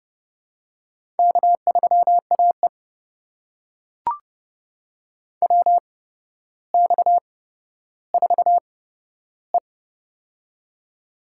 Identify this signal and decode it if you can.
Morse code, heard as a keyed tone, about 32 wpm, spaced as for 13 wpm, 700 Hz. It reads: K3AE WX4E
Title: USA Call Signs - 4 Characters - 8x Character Spacing - 30wpm